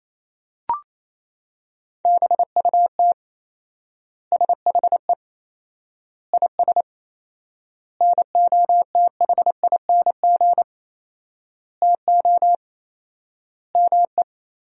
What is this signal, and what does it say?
Morse code, heard as a keyed tone, about 28 wpm, 700 Hz. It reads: BUT SHE IS NOTHING TO ME
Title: 4x Word-Spacing - Sentences from Top 500 Words - 28wpm